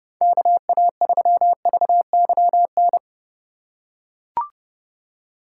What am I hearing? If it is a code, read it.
Morse code, heard as a keyed tone, about 30 wpm, 700 Hz. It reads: KA3VYN